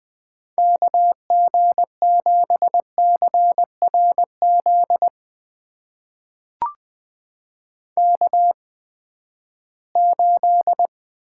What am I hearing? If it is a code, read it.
Morse code, heard as a keyed tone, about 20 wpm, spaced as for 14 wpm, 700 Hz. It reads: KG7CRZ K8
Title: USA Call Signs - 8x Character Spacing - 20wpm